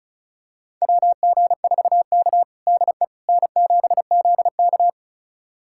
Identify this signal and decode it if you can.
Morse code, heard as a keyed tone, about 36 wpm, 700 Hz. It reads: WG4K DE N7ZK
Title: USA Callsign DE Callsign - 4 Characters - 35wpm